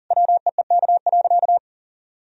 Morse code, heard as a keyed tone, about 40 wpm, 700 Hz. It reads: WEEK.
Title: Sentences from Top 1000 Words - Review Part 2 - 40wpm